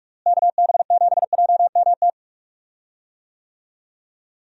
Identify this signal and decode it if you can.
Morse code, heard as a keyed tone, about 46 wpm, 700 Hz. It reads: KB7JMT
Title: USA Call Signs - 6 Characters - Repeated 3x - 45wpm